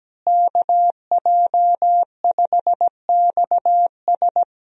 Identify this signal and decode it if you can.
Morse code, heard as a keyed tone, about 17 wpm, 700 Hz. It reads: KJ5XS
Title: USA Call Signs - 5 Characters - Repeated 2x - 17wpm